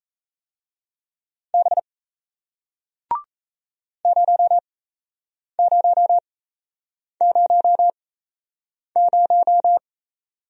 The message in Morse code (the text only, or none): B 0000